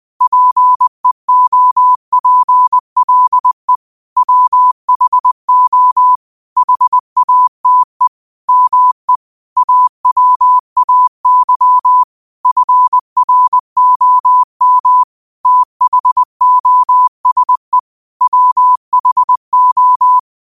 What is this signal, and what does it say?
Morse code, heard as a keyed tone, about 20 wpm, 1000 Hz. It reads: PEOPLE WHO HATE ME AWAY FROM THOSE WHO
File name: qotdfiles / 20wpm